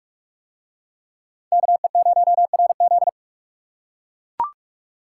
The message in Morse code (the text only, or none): KE0RZ